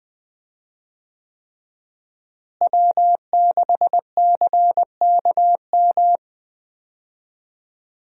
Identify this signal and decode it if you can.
Morse code, heard as a keyed tone, about 20 wpm, 700 Hz. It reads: W6CKM
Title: USA Call Signs - 5 Characters - Repeated 3x - 20wpm